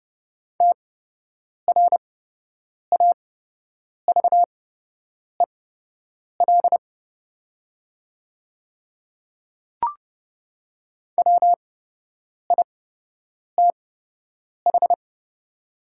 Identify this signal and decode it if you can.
Morse code, heard as a keyed tone, about 31 wpm, spaced as for 8 wpm, 700 Hz. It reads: TRAVEL WITH
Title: 8x Character Spacing - Top 500 Words - 30wpm